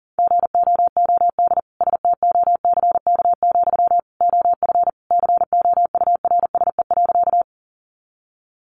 Morse code, heard as a keyed tone, about 40 wpm, 700 Hz. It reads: GOOD STOCK, OF COURSE.
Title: Sentences from Top 800 Words 40wpm